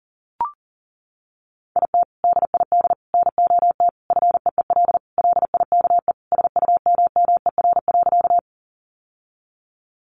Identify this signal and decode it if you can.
Morse code, heard as a keyed tone, about 40 wpm, 700 Hz. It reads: IT DID NOT FEEL LIKE SUMMER.
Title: Sentences from Top 1100 Words 40wpm